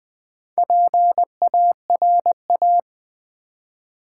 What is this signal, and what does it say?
Morse code, heard as a keyed tone, about 20 wpm, 700 Hz. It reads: PARA
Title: Build A Word - 2 Letters At A Time - 8-Letter Words - 20wpm